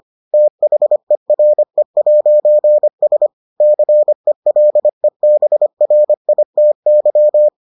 Morse code, heard as a keyed tone, about 25 wpm, 600 Hz. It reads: THERE'S CELEBRITY